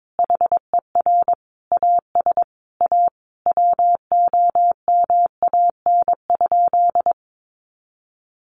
Morse code, heard as a keyed tone, about 22 wpm, 700 Hz. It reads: HER AS A WOMAN?
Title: Sentences from Top 1100 Words 22wpm